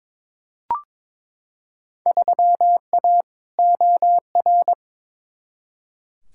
Morse code, heard as a keyed tone, about 22 wpm, 700 Hz. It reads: 3A OR
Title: Field Day Exchanges Encore - 22wpm